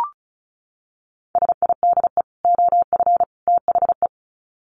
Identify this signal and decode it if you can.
Morse code, heard as a keyed tone, about 34 wpm, 700 Hz. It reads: SIDE OF THE
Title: Sets of 3 Words 35wpm